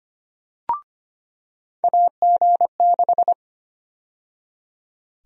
Morse code, heard as a keyed tone, about 25 wpm, 700 Hz. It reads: AG6
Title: USA Call Sign Prefixes - 25wpm